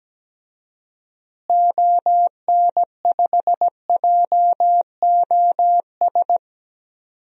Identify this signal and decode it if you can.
Morse code, heard as a keyed tone, about 17 wpm, 700 Hz. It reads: ON5JOS
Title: DXCC-100 Call Signs - 6 Characters - Repeated 2x - 17wpm